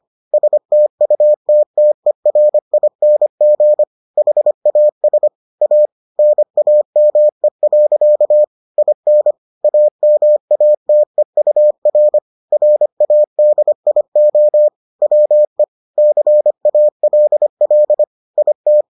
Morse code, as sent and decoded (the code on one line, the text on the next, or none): ... - ..- - - . .-. .. -. --. / .... .- ... / .- / -. .- -- . .-.-.- / .. -. / .- -- .- - . ..- .-. / .-. .- -.. .. --- / .-- . / -.-. .- .-.. .-.. / .. -
STUTTERING HAS A NAME. IN AMATEUR RADIO WE CALL IT